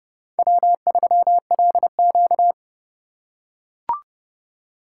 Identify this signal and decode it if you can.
Morse code, heard as a keyed tone, about 29 wpm, 700 Hz. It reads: W3LQ